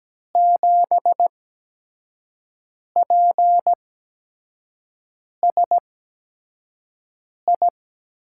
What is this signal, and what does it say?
Morse code, heard as a keyed tone, about 17 wpm, spaced as for 5 wpm, 700 Hz. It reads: 7PSI
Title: USA Call Signs - 6 Characters - 8x Character Spacing - 17wpm